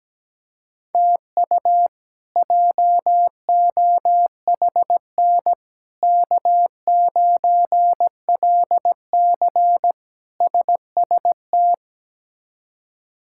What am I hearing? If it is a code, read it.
Morse code, heard as a keyed tone, about 17 wpm, 700 Hz. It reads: TU JOHN K9LC SST
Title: K1USN SST QSO Practice - 17wpm